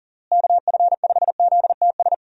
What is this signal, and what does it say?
Morse code, heard as a keyed tone, about 40 wpm, 700 Hz. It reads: KF5ZTS
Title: USA Call Signs - 6 Characters - Repeated 3x - 40wpm